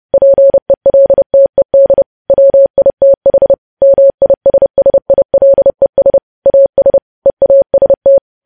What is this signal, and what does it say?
Morse code, heard as a keyed tone, about 30 wpm, 550 Hz. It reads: PELTED WITH MISSILES AS EAST